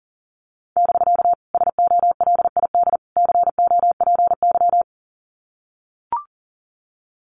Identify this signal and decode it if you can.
Morse code, heard as a keyed tone, about 40 wpm, 700 Hz. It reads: <BK> SOLID COPY
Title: Common QSO Phrases Encore 40wpm